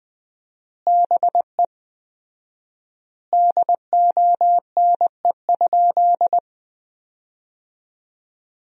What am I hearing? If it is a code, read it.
Morse code, heard as a keyed tone, about 20 wpm, 700 Hz. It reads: BE DONE?
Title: 4x Word-Spacing - Sentences from Top 500 Words - 20wpm